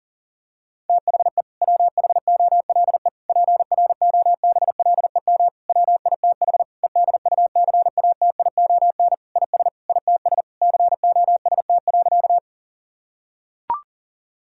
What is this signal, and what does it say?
Morse code, heard as a keyed tone, about 40 wpm, 700 Hz. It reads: THE WHOLE PROBLEM WITH EDUCATION IS ITS COST.
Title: Sentences from Top 1000 Words - Review Part 6 - 40wpm